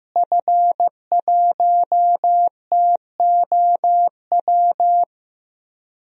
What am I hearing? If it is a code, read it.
Morse code, heard as a keyed tone, about 15 wpm, 700 Hz. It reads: F1TOW
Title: DXCC-100 Call Signs - 5 Characters - Repeated 2x - 15wpm